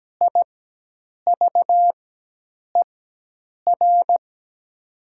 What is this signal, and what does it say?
Morse code, heard as a keyed tone, about 17 wpm, spaced as for 8 wpm, 700 Hz. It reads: IVER
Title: 4x Character Spacing - Top 300 Words - 17wpm